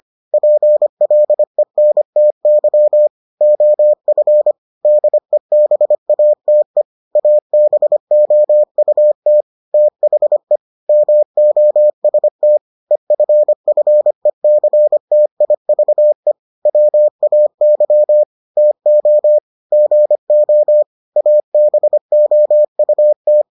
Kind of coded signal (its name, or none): Morse code